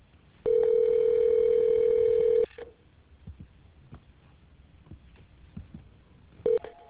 Ringback tone (North America), 440 + 480 Hz.